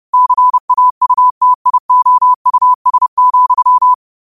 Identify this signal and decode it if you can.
Morse code, heard as a keyed tone, about 30 wpm, 1000 Hz. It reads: CAUTIOUS,
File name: qotdfiles / 30wpm